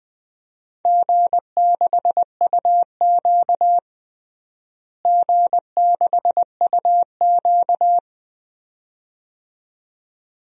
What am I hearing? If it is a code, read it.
Morse code, heard as a keyed tone, about 21 wpm, 700 Hz. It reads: G6UQ G6UQ
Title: DXCC-100 Call Signs - 4 Characters - Repeated 2x - 20wpm